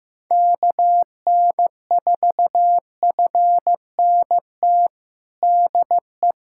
Morse code, heard as a keyed tone, about 15 wpm, 700 Hz. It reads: KN4FNT DE
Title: USA Callsign DE Callsign - 6 Characters - 15wpm